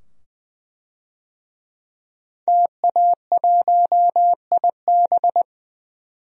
Morse code, heard as a keyed tone, about 20 wpm, 700 Hz. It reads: TA1IB
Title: DXCC-100 Call Signs - 5 Characters - 20wpm